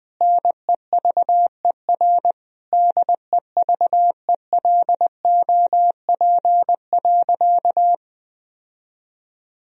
Morse code, heard as a keyed tone, about 19 wpm, 700 Hz. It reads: NEVER DEVELOP.
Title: Sentences from Top 1000 Words - Review Part 2 - 20wpm